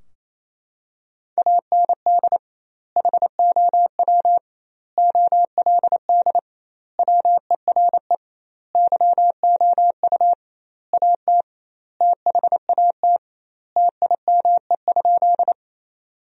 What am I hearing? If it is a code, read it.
Morse code, heard as a keyed tone, about 27 wpm, 700 Hz. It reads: AND HOW OLD WERE YOU AT THAT TIME?
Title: 2x Word-Spacing - Sentences from Top 200 Words - 28wpm